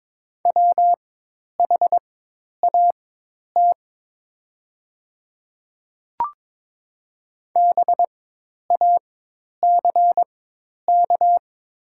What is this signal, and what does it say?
Morse code, heard as a keyed tone, about 23 wpm, spaced as for 10 wpm, 690 Hz. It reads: WHAT BACK
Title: 4x Character Spacing - Top 200 Words - 22wpm